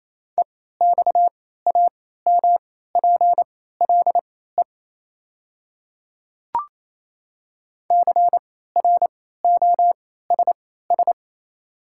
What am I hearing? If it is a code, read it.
Morse code, heard as a keyed tone, about 28 wpm, spaced as for 16 wpm, 690 Hz. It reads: EXAMPLE CROSS
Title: 3x Character Spacing - Top 500 Words - 28wpm